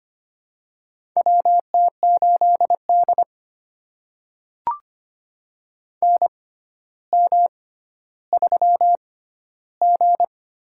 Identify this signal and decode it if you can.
Morse code, heard as a keyed tone, about 25 wpm, spaced as for 13 wpm, 700 Hz. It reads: WT8D NM3G
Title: USA Call Signs - 4 Characters - 6x Character Spacing - 25wpm